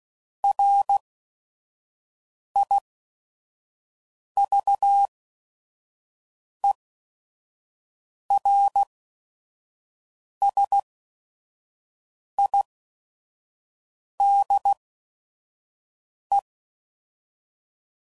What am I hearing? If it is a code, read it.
Morse code, heard as a keyed tone, about 16 wpm, spaced as for 5 wpm, 800 Hz. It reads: RIVERSIDE
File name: CW Code Groups @5WPM / fromSM's / Cities